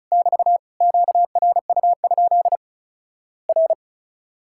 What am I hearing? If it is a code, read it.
Morse code, heard as a keyed tone, about 36 wpm, 700 Hz. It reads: <BT> QRU? R
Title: Worked All Europe DX Contest - QTC Exchange - 35wpm